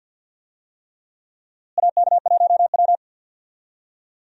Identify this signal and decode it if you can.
Morse code, heard as a keyed tone, about 48 wpm, 700 Hz. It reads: AK1W
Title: USA Call Signs - 4 Characters - Repeated 3x - 50wpm